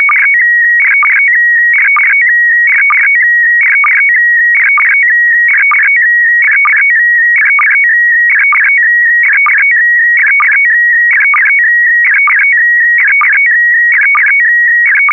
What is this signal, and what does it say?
SSTV (slow-scan television), PD 290.